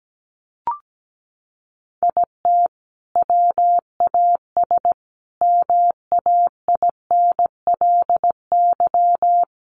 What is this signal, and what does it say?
Morse code, heard as a keyed tone, about 17 wpm, 700 Hz. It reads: IT WAS MAINLY
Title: Sentences from Top 1400 Words 17wpm